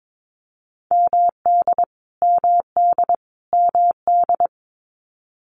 Morse code, heard as a keyed tone, about 22 wpm, 700 Hz. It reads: MD MD MD
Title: US States Repeated 3x 22wpm